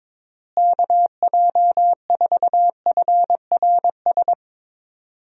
Morse code, heard as a keyed tone, about 22 wpm, 700 Hz. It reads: KJ4FRS
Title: USA Call Signs - 6 Characters - Repeated 3x - 22wpm